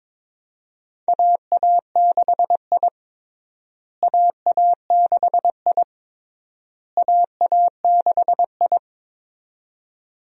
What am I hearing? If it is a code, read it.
Morse code, heard as a keyed tone, about 22 wpm, 700 Hz. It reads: AA6I AA6I AA6I
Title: USA Call Signs - 4 Characters - Repeated 3x - 22wpm